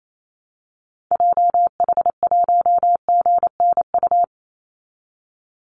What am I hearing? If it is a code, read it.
Morse code, heard as a keyed tone, about 28 wpm, 700 Hz. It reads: JH1GNU
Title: Call Signs Repeated 2x - 28wpm